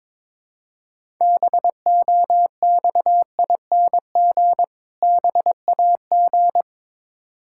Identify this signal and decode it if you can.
Morse code, heard as a keyed tone, about 22 wpm, 690 Hz. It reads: BOXING BAG
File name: Difficult Words with Hints - 22wpm